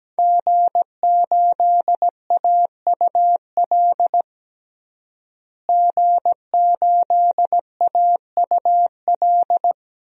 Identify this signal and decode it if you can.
Morse code, heard as a keyed tone, about 17 wpm, 700 Hz. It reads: G8AUL G8AUL
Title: DXCC-100 Call Signs - 5 Characters - Repeated 2x - 17wpm